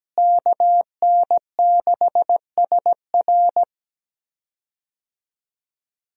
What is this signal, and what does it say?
Morse code, heard as a keyed tone, about 17 wpm, 700 Hz. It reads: KN6SR
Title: USA Call Signs - 5 Characters - Repeated 2x - 17wpm